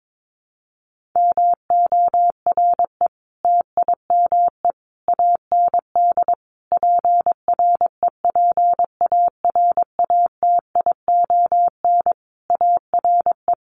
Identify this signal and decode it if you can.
Morse code, heard as a keyed tone, about 22 wpm, 700 Hz. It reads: MORE TIME AND PREPARATION ARE
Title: Sentences from Top 1400 Words 22wpm